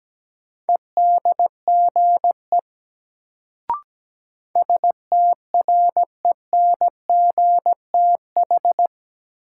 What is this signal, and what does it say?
Morse code, heard as a keyed tone, about 17 wpm, 700 Hz. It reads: EDGE STRENGTH
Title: Sentences from Top 1300 Words - New Word Drill Alt - 17wpm